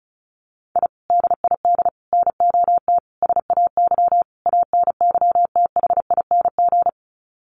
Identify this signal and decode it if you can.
Morse code, heard as a keyed tone, about 34 wpm, 700 Hz. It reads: I DID NOT SAY ANYTHING
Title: Sets of 5 Words Encore2 35wpm